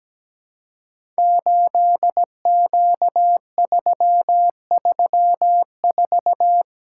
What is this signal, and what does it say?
Morse code, heard as a keyed tone, about 17 wpm, 700 Hz. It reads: 8Q334
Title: Code Groups - 5 Letter:Numbers - 17wpm